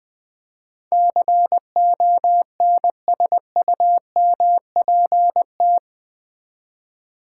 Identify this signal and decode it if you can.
Morse code, heard as a keyed tone, about 20 wpm, 700 Hz. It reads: CONSUMPT
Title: Build A Word - 4 Letters At A Time - 11-Letter Words - 20wpm